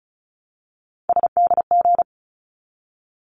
Morse code, heard as a keyed tone, about 34 wpm, 700 Hz. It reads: SDG
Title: ARRL Sections 35wpm